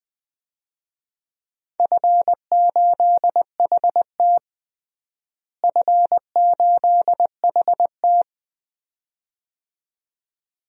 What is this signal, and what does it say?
Morse code, heard as a keyed tone, about 20 wpm, 700 Hz. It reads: F8HT F8HT
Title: DXCC-100 Call Signs - 4 Characters - Repeated 2x - 20wpm